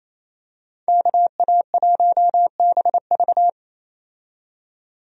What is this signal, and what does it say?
Morse code, heard as a keyed tone, about 28 wpm, 700 Hz. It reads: KA1BV
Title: USA Call Signs - 5 Characters - 28wpm